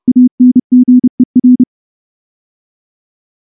Morse code, heard as a keyed tone, about 30 wpm, 250 Hz. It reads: ANGER
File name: morse-training-30WPM